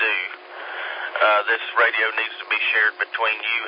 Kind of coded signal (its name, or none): Emergency Alert System (SAME) tones